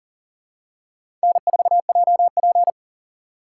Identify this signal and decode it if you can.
Morse code, heard as a keyed tone, about 39 wpm, 700 Hz. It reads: N4JP
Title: USA Call Signs - 4 Characters - 3x Character Spacing - 40wpm